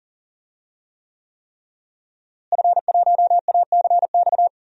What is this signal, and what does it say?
Morse code, heard as a keyed tone, about 40 wpm, 700 Hz. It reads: F1ACX
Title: DXCC-100 Call Signs - 5 Characters - Repeated 3x - 40wpm